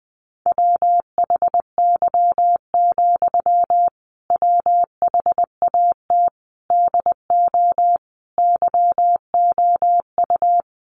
Morse code, heard as a keyed tone, about 20 wpm, 700 Hz. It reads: WHY, WHAT DO YOU
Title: Sentences from Top 200 Words 20wpm